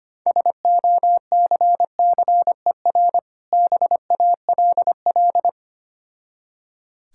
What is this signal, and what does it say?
Morse code, heard as a keyed tone, about 25 wpm, 700 Hz. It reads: SOCCER BALL
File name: practice / 25wpm / Difficult Words with Hints - 25wpm